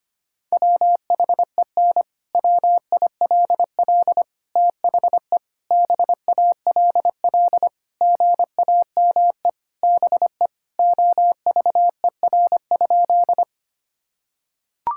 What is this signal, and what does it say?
Morse code, heard as a keyed tone, about 25 wpm, 700 Hz. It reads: WHEN WILL THE BALL GAME BE OVER?
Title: Sentences from Top 1000 Words - Review Part 6 - 25wpm